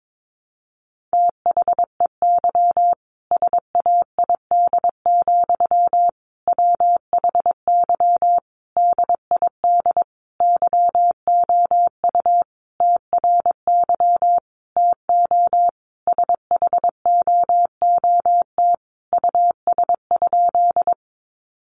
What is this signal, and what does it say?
Morse code, heard as a keyed tone, about 22 wpm, 700 Hz. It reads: THEY SAID, WHY DID YOU TRY TO SHOOT US?
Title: Sentences from Top 1200 Words 22wpm